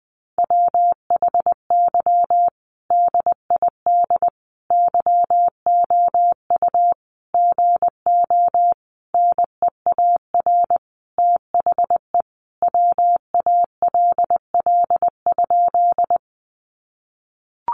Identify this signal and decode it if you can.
Morse code, heard as a keyed tone, about 20 wpm, 700 Hz. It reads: WHY DID YOU GO NEAR THE WALL?
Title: Sentences from Top 1100 Words 20wpm